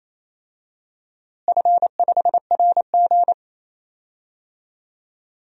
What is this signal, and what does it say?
Morse code, heard as a keyed tone, about 28 wpm, 700 Hz. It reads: F5RG
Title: DXCC-100 Call Signs - 4 Characters - Repeated 3x - 28wpm